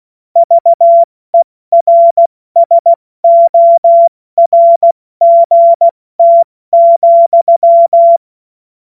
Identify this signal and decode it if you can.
Morse code, heard as a keyed tone, about 16 wpm, 680 Hz. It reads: VERSORGT,
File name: Morsekurs / Witze / Witz 02 April 2024 70 F80